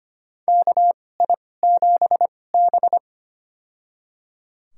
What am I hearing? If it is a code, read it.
Morse code, heard as a keyed tone, about 25 wpm, spaced as for 18 wpm, 700 Hz. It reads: KI7B
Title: USA Call Signs - 4 Characters - 2x Character Spacing - 25wpm